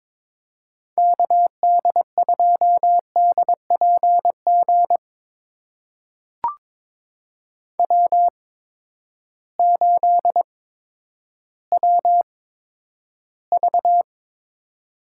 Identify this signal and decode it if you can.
Morse code, heard as a keyed tone, about 23 wpm, spaced as for 11 wpm, 700 Hz. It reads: KD2DPG W8WV
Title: USA Call Signs - 8x Character Spacing - 22wpm